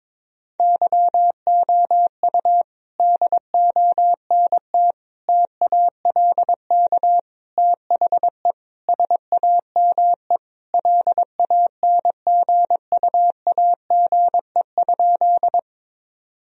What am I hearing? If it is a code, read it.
Morse code, heard as a keyed tone, about 21 wpm, 700 Hz. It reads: YOU DONT TALK THE SAME LANGUAGE?
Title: Sentences from Top 1000 Words - Review Part 3 - 22wpm